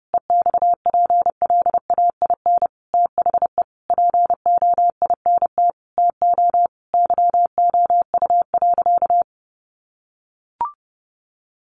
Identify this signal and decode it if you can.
Morse code, heard as a keyed tone, about 30 wpm, 700 Hz. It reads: EXPLAIN THE POINT TO YOU.
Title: Sentences from Top 1100 Words 30wpm